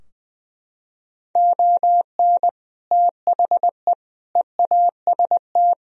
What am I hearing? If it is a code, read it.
Morse code, heard as a keyed tone, about 21 wpm, 700 Hz. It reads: ON THE EAST
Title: Sentences from Top 1000 Words - Review Part 4 - 20wpm